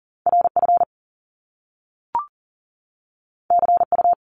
Morse code, heard as a keyed tone, about 40 wpm, 700 Hz. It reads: RF CU
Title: Top Two Letter Combinations in Words 40wpm